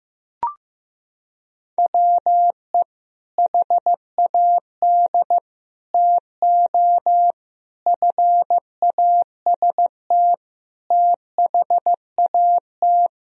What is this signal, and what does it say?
Morse code, heard as a keyed tone, about 15 wpm, 700 Hz. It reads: WE HAD TO FAST THAT
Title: Sentences from Top 1000 Words - Review Part 2 - 15wpm